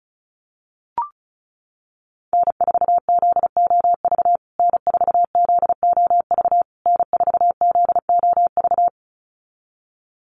Morse code, heard as a keyed tone, about 34 wpm, 700 Hz. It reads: N4ZOV N4ZOV N4ZOV
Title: USA Call Signs Repeated 3x 35wpm